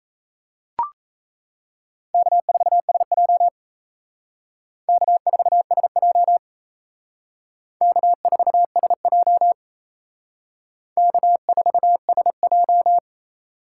Morse code, heard as a keyed tone, about 36 wpm, 700 Hz. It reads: K4SJ K4SJ K4SJ K4SJ